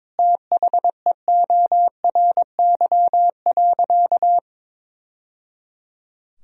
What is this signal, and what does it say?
Morse code, heard as a keyed tone, about 22 wpm, 700 Hz. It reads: THEORY.
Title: Sentences from Top 1000 Words - Review Part 3 - 22wpm